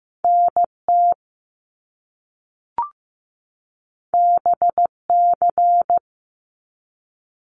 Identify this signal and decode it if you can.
Morse code, heard as a keyed tone, about 15 wpm, 700 Hz. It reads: NT BC
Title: Canadian Provinces and Territories 15wpm